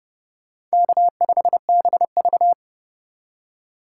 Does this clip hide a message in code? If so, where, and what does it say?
Yes: Morse code from 0.7 to 2.5 s, heard as a keyed tone, about 40 wpm, 700 Hz. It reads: K5BV